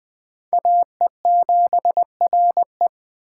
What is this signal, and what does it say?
Morse code, heard as a keyed tone, about 20 wpm, 700 Hz. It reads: AE7RE
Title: Speed Racing - USA 2x2 Call Signs 17wpm